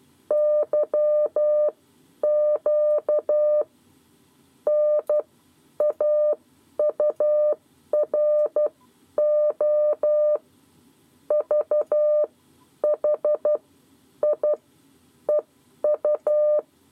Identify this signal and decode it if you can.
Morse code, heard as a keyed tone, about 12 wpm, spaced as for 10 wpm, 600 Hz. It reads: YQ N AURO V H I EU
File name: 7 Lezione cw 2